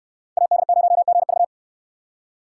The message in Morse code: AD0GV

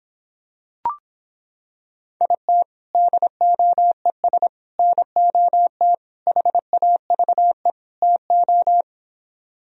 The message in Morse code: IT DOES NOT HAVE TO

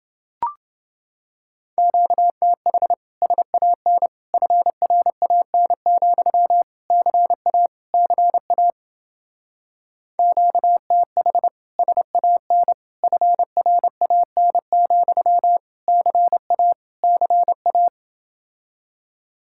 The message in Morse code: QTH SAN FRAN, CA CA QTH SAN FRAN, CA CA

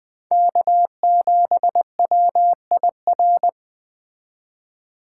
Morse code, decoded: K7WIR

